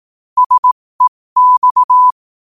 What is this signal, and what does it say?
Morse code, heard as a keyed tone, about 18 wpm, spaced as for 15 wpm, 1000 Hz. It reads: SEX
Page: 15 WPM morse code quotes